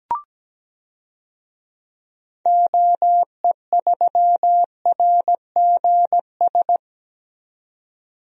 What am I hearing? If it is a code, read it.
Morse code, heard as a keyed tone, about 17 wpm, 700 Hz. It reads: OE3RGS
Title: DXCC-100 Call Signs - 6 Characters - 17wpm